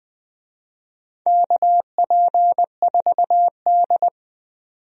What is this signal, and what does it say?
Morse code, heard as a keyed tone, about 20 wpm, 700 Hz. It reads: KP4D
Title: DXCC-100 Call Signs - 4 Characters - 20wpm